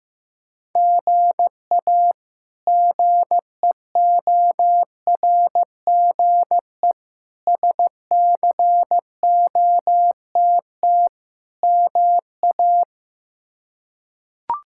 Morse code, heard as a keyed tone, about 15 wpm, 700 Hz. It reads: GA GEORGE SCOTT MA
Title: K1USN SST QSO Practice - 15wpm